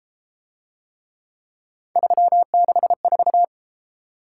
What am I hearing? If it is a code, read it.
Morse code, heard as a keyed tone, about 32 wpm, 700 Hz. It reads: 364